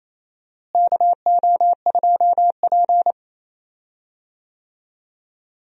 Morse code, heard as a keyed tone, about 28 wpm, 700 Hz. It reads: KO2P
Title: USA Call Signs - 4 Characters - Repeated 2x - 28wpm